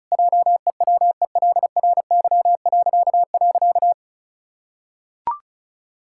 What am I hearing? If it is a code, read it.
Morse code, heard as a keyed tone, about 34 wpm, 700 Hz. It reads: JEWELRY..
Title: Pangram Challenge v2 - 35wpm